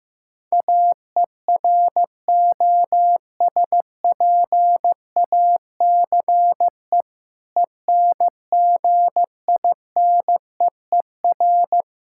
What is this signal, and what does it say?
Morse code, heard as a keyed tone, about 15 wpm, 700 Hz. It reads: AEROSPACE ENGINEER
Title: K1USN SST QSO Practice 15wpm